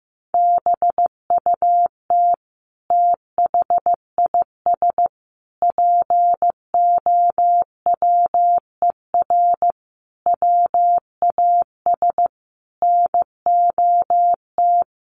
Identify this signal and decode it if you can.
Morse code, heard as a keyed tone, about 15 wpm, 700 Hz. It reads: BUT THIS POWER WAS NOT
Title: Sentences from Top 1500 Words 15wpm